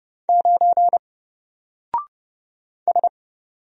Morse code, heard as a keyed tone, about 30 wpm, 700 Hz. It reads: 9 S
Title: Single Letter-Number Alt 30wpm